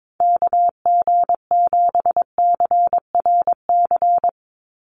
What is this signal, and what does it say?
Morse code, heard as a keyed tone, about 22 wpm, 700 Hz. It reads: KG7CRC
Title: USA Call Signs 22wpm